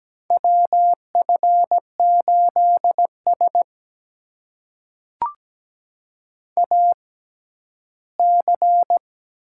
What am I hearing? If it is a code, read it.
Morse code, heard as a keyed tone, about 17 wpm, spaced as for 11 wpm, 700 Hz. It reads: WF8S AC
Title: USA Call Signs - 6x Character Spacing - 17wpm